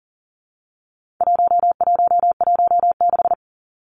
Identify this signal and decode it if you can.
Morse code, heard as a keyed tone, about 40 wpm, 700 Hz. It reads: 1116
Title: Numbers - 4-Digit 40wpm